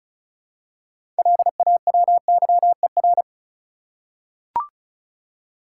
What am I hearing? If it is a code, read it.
Morse code, heard as a keyed tone, about 34 wpm, 700 Hz. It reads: LAWYER
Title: Six Letter Words - 35wpm